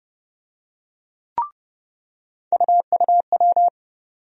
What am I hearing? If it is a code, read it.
Morse code, heard as a keyed tone, about 30 wpm, 700 Hz. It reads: UUW
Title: USA Call Sign Postfixes - 3 Letters 30wpm